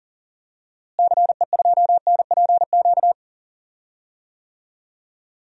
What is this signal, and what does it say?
Morse code, heard as a keyed tone, about 40 wpm, 700 Hz. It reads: CE2NPQ
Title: DXCC-100 Call Signs - 6 Characters - Repeated 3x - 40wpm